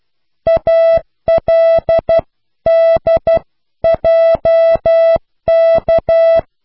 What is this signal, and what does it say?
Morse code, heard as a keyed tone, about 12 wpm, 660 Hz. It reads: ALDJK
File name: Alles lage snelheid